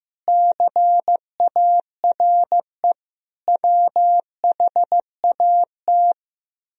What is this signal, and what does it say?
Morse code, heard as a keyed tone, about 15 wpm, 700 Hz. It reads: CARE WHAT
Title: Speed-Racing - Sets of 5 Words 15wpm